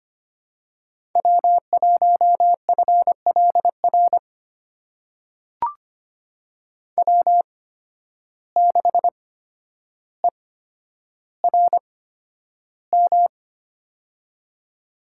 Morse code, heard as a keyed tone, about 25 wpm, spaced as for 11 wpm, 700 Hz. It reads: W1FLR W6ERM
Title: USA Call Signs - 5 Characters - 8x Character Spacing - 25wpm